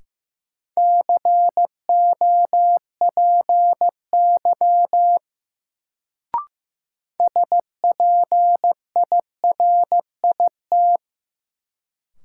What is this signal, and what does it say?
Morse code, heard as a keyed tone, about 15 wpm, 700 Hz. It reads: COPY SPIRIT
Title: Sentences from Top 1400 Words - New Word Drill - 15wpm